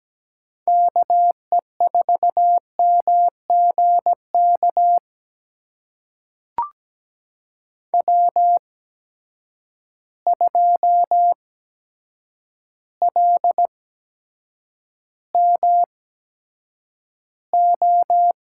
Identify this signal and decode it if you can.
Morse code, heard as a keyed tone, about 17 wpm, spaced as for 8 wpm, 700 Hz. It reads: KE4MGK W2LMO
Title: USA Call Signs - 8x Character Spacing - 17wpm